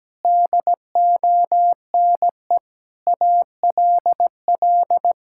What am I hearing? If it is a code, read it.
Morse code, heard as a keyed tone, about 17 wpm, 700 Hz. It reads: DONE ALL
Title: Sentences from Top 1000 Words - Review Part 5 - 17wpm